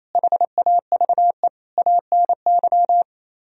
Morse code, heard as a keyed tone, about 35 wpm, 700 Hz. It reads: HAVE ANY
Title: Speed-Racing - Sets of 4 Words 28wpm